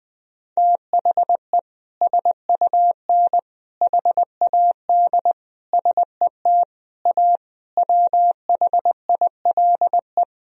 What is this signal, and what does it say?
Morse code, heard as a keyed tone, about 21 wpm, 700 Hz. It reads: THE SUN HAD SET A WHILE
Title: Sentences from Top 1000 Words - Review Part 1 - 20wpm